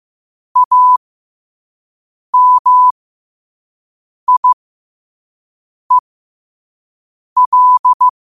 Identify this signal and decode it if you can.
Morse code, heard as a keyed tone, about 15 wpm, spaced as for 5 wpm, 1000 Hz. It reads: AMIEL
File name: qotdfiles / 5wpm